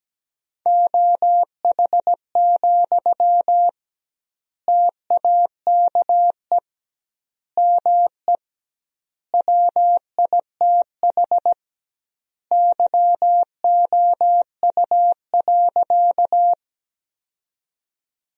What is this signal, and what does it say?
Morse code, heard as a keyed tone, about 17 wpm, 700 Hz. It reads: OH, TAKE ME WITH YOU.
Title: 2x Word-Spacing - Sentences from Top 500 Words - 17wpm